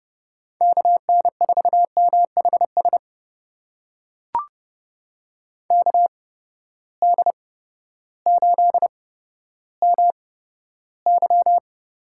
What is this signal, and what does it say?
Morse code, heard as a keyed tone, about 30 wpm, spaced as for 14 wpm, 700 Hz. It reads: KN4MHS KD8MY
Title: USA Call Signs - 6 Characters - 8x Character Spacing - 30wpm